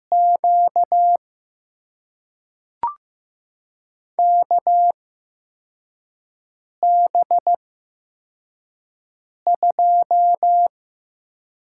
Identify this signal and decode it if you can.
Morse code, heard as a keyed tone, about 15 wpm, spaced as for 4 wpm, 700 Hz. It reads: Q KB2
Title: USA Call Signs - 6 Characters - 8x Character Spacing - 15wpm